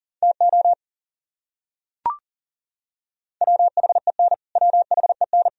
Morse code, heard as a keyed tone, about 40 wpm, 700 Hz. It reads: TO WHEN WHEN
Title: Top 100 Words Repeated 40wpm